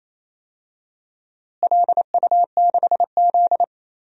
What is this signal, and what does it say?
Morse code, heard as a keyed tone, about 28 wpm, 700 Hz. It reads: LU6Z
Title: DXCC-100 Call Signs - 4 Characters - Repeated 2x - 28wpm